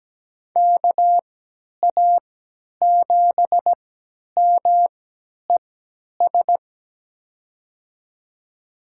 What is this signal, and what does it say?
Morse code, heard as a keyed tone, about 17 wpm, spaced as for 10 wpm, 700 Hz. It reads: KA7MES
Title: USA Call Signs - 6 Characters - 3x Character Spacing - 17wpm